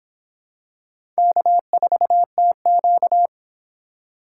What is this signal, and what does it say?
Morse code, heard as a keyed tone, about 26 wpm, 700 Hz. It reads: K4TQ